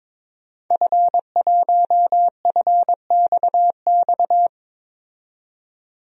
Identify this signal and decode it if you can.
Morse code, heard as a keyed tone, about 22 wpm, 700 Hz. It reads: F1FXX
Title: DXCC-100 Call Signs - 5 Characters - 22wpm